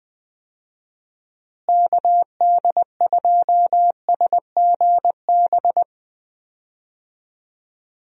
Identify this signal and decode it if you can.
Morse code, heard as a keyed tone, about 20 wpm, 700 Hz. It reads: KD2SGB
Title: USA Call Signs - 6 Characters - Repeated 2x - 20wpm